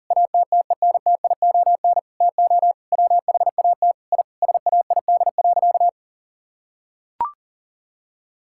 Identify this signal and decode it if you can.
Morse code, heard as a keyed tone, about 40 wpm, 700 Hz. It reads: ATTENTION TO WHAT I SAID.
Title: Sentences from Top 1000 Words - Review Part 6 - 40wpm